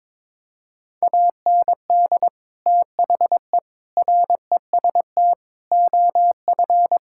Morse code, heard as a keyed tone, about 22 wpm, 700 Hz. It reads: AND THE REST OF
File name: practice / 22wpm / Sentences with 4-Word Phrases - 22wpm